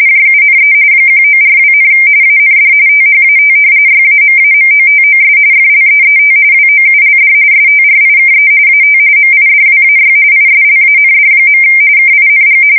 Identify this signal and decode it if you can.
RTTY, about 45.45 baud. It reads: JUNE 1942, AND THAT YAMAMOTOS FLEET WOULD STRIKE AT MIDWAY ISLAND. KING
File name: fsk170 / usnavy4